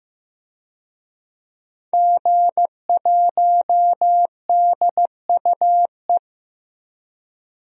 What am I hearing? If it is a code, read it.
Morse code, heard as a keyed tone, about 15 wpm, 700 Hz. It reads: G1DUE
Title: DXCC-100 Call Signs - 5 Characters - 15wpm